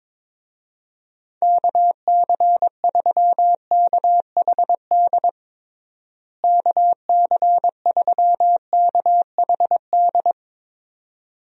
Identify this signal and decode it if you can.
Morse code, heard as a keyed tone, about 22 wpm, 700 Hz. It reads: KC3KHD KC3KHD
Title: USA Call Signs - 6 Characters - Repeated 2x - 22wpm